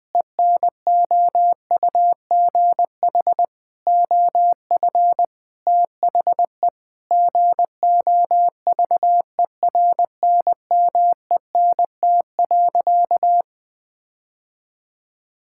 Morse code, heard as a keyed tone, about 20 wpm, 700 Hz. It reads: ENOUGH OF THE GOVERNMENT.
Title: Sentences from Top 1600 Words 20wpm